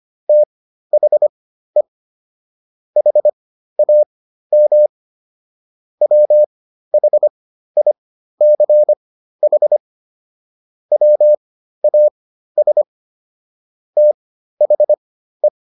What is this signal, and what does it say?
Morse code, heard as a keyed tone, about 25 wpm, spaced as for 13 wpm, 600 Hz. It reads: THE HAM WHICH WAS THE
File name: SS109 13